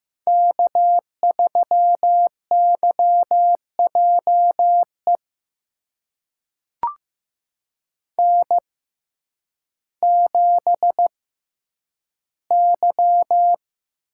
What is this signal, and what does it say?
Morse code, heard as a keyed tone, about 15 wpm, spaced as for 9 wpm, 700 Hz. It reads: K3YJE N7Y
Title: USA Call Signs - 5 Characters - 6x Character Spacing - 15wpm